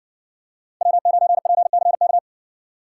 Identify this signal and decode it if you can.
Morse code, heard as a keyed tone, about 59 wpm, 700 Hz. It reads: W9PCK